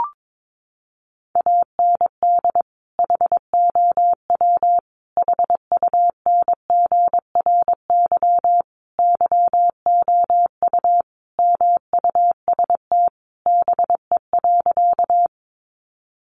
Morse code, heard as a keyed tone, about 22 wpm, 700 Hz. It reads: AND HOW HUNGRY YOU MUST BE.
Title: Sentences from Top 1400 Words 22wpm